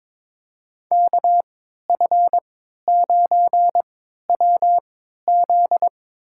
Morse code, heard as a keyed tone, about 22 wpm, spaced as for 12 wpm, 700 Hz. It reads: KF9WZ
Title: USA Call Signs - 5 Characters - 3x Character Spacing - 22wpm